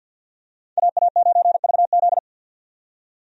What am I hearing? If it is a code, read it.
Morse code, heard as a keyed tone, about 51 wpm, 700 Hz. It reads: AA9VZ
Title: USA Call Signs - 5 Characters - 50wpm